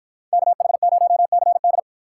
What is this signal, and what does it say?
Morse code, heard as a keyed tone, about 53 wpm, 700 Hz. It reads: KH0CD